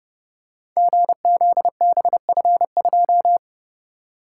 Morse code, heard as a keyed tone, about 30 wpm, 700 Hz. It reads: GZBF2
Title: Code Groups - 5 Letter:Numbers - 30wpm